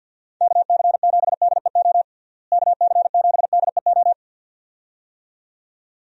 Morse code, heard as a keyed tone, about 52 wpm, 700 Hz. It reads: KC7DEO KC7DEO
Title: USA Call Signs - 6 Characters - Repeated 2x - 50wpm